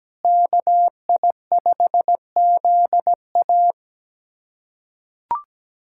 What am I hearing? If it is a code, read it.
Morse code, heard as a keyed tone, about 17 wpm, 700 Hz. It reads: KI5ZA
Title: USA Callsign DE Callsign - 5 Characters - 17wpm